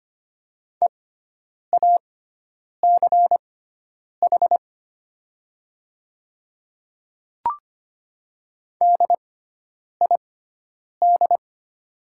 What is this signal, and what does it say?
Morse code, heard as a keyed tone, about 25 wpm, spaced as for 9 wpm, 700 Hz. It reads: EACH DID
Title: 6x Character Spacing - Top 100 Words - 25wpm